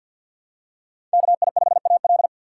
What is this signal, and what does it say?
Morse code, heard as a keyed tone, about 49 wpm, 700 Hz. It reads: KI5AL
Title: K1USN SST QSO Practice - 50wpm